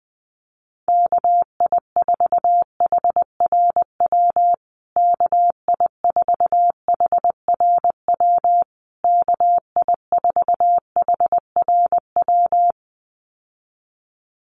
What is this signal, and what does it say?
Morse code, heard as a keyed tone, about 20 wpm, 700 Hz. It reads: KI4HRW KI4HRW KI4HRW
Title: USA Call Signs Repeated 3x 20wpm